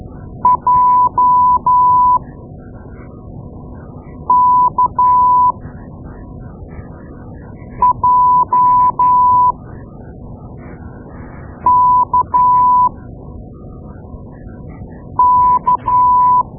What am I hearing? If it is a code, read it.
Morse code, heard as a keyed tone, about 13 wpm, spaced as for 4 wpm, 1000 Hz. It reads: JKJKK
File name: Morse Lessons / Morse Lessons / Tape-2A